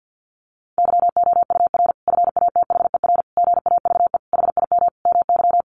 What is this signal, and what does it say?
Morse code, heard as a keyed tone, about 50 wpm, 700 Hz. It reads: YOUR FATHER GAVE HIM MY
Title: Sentences from Top 1400 Words 50wpm